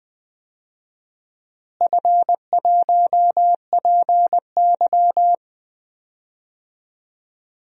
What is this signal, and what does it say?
Morse code, heard as a keyed tone, about 21 wpm, 700 Hz. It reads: F1PY
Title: DXCC-100 Call Signs - 4 Characters - Repeated 3x - 20wpm